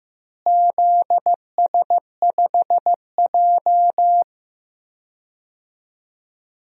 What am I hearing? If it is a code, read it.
Morse code, heard as a keyed tone, about 15 wpm, 700 Hz. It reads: ZS5J
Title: DXCC-100 Call Signs - 4 Characters - Repeated 3x - 15wpm